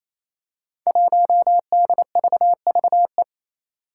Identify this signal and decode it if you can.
Morse code, heard as a keyed tone, about 28 wpm, 700 Hz. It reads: 1DVVE
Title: Five Challenging Letter-Number Combinations 28wpm